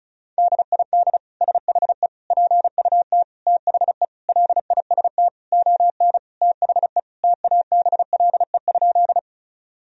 Morse code, heard as a keyed tone, about 34 wpm, 700 Hz. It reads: DID SHE PUT THE LIST ON THE TABLE?
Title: Sentences from Top 1000 Words - Review Part 2 - 35wpm